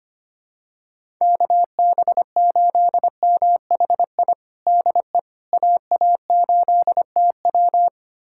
Morse code, heard as a keyed tone, about 25 wpm, 700 Hz. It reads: KB8MHI DE AA8TW
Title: USA Callsign DE Callsign 25wpm